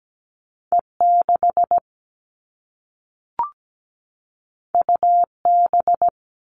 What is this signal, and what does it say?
Morse code, heard as a keyed tone, about 17 wpm, 700 Hz. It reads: E6 UB
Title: All Cut Numbers - 2-Digit 17wpm